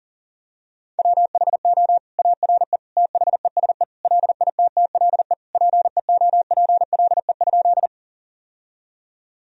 Morse code, heard as a keyed tone, about 40 wpm, 700 Hz. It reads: WHO ARE THESE LITTLE PEOPLE?
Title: Sentences from Top 1000 Words - Review Part 1 - 40wpm